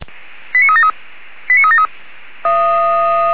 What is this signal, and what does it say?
Five-tone selective call (ZVEI), heard as quick runs of five tones, tones of about 70 ms. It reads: 89373 89373
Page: SDR/ZVEI